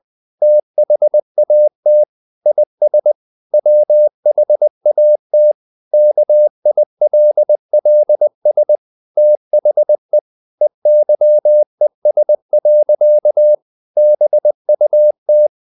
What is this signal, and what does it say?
Morse code, heard as a keyed tone, about 20 wpm, 600 Hz. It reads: THAT IS WHAT KILLS THE EYES. BUT
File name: the old man and the sea / the old man and the sea part 1 20wpm